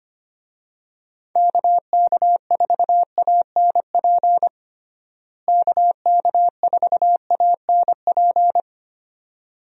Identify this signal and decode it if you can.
Morse code, heard as a keyed tone, about 25 wpm, 700 Hz. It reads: KK4ANP KK4ANP
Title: USA Call Signs - 6 Characters - Repeated 2x - 25wpm